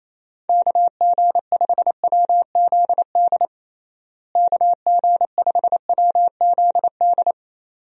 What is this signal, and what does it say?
Morse code, heard as a keyed tone, about 28 wpm, 700 Hz. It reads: KG5WZD KG5WZD
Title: USA Call Signs - 6 Characters - Repeated 2x - 28wpm